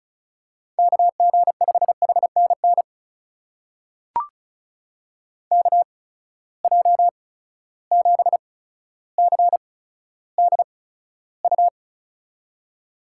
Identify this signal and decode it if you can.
Morse code, heard as a keyed tone, about 36 wpm, spaced as for 15 wpm, 700 Hz. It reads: KG5HNN KJ7CDU
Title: USA Call Signs - 6 Characters - 8x Character Spacing - 35wpm